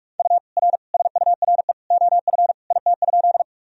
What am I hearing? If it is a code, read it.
Morse code, heard as a keyed tone, about 46 wpm, 700 Hz. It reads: U R SURE OF IT?
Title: Sentences from Top 300 Words and Abbreviations 45wpm